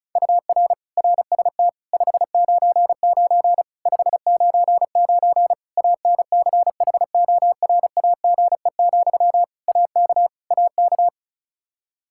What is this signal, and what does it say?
Morse code, heard as a keyed tone, about 36 wpm, 700 Hz. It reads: UR RST 599 599 ANCHORAGE, AK AK
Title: Practice QSO - Exchange 1 - Beginning - 35wpm